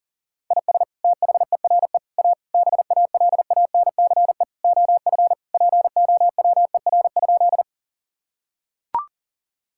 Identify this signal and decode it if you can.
Morse code, heard as a keyed tone, about 40 wpm, 700 Hz. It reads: IS THERE A BALANCE OF POWER?
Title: Sentences from Top 1000 Words - Review Part 6 - 40wpm